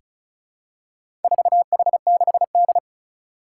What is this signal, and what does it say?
Morse code, heard as a keyed tone, about 36 wpm, 700 Hz. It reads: 4H6D
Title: Four Challenging Letter-Number Combinations 35wpm